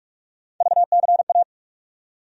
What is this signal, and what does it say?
Morse code, heard as a keyed tone, about 46 wpm, 700 Hz. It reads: VCA
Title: USA Call Sign Postfixes - 3 Letters 45wpm